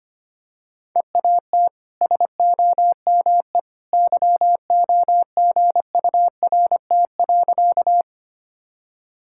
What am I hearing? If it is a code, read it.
Morse code, heard as a keyed tone, about 25 wpm, 700 Hz. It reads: EAT SOME YOGURT.
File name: Easy Sentences with Hints - 25wpm